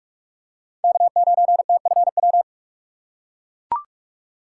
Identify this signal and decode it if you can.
Morse code, heard as a keyed tone, about 42 wpm, 700 Hz. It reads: K9TFW
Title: USA Call Signs - 5 Characters - 2x Character Spacing - 45wpm